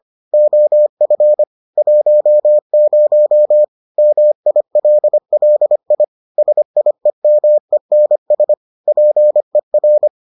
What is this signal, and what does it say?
Morse code, heard as a keyed tone, about 25 wpm, 600 Hz. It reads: OF 10 MILLI SIEMENS PER